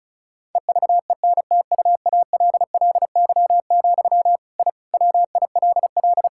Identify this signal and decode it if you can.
Morse code, heard as a keyed tone, about 36 wpm, 700 Hz. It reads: EVENTUALLY, I WILL
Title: Sentences from Top 1000 Words - Review Part 6 - 35wpm